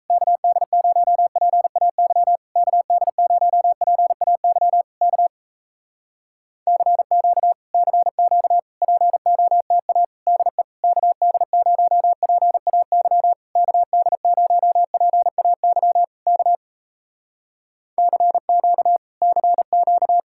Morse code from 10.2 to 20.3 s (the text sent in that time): DE KD0PAY KD0PAY K CQ CQ